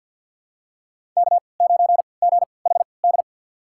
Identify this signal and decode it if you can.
Morse code, heard as a keyed tone, about 49 wpm, spaced as for 28 wpm, 690 Hz. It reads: K9GHD
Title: USA Call Signs - 5 Characters - 3x Character Spacing - 50wpm